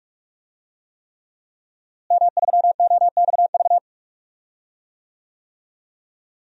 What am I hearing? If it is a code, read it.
Morse code, heard as a keyed tone, about 46 wpm, 700 Hz. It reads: M3OXV
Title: DXCC-100 Call Signs - 5 Characters - Repeated 3x - 45wpm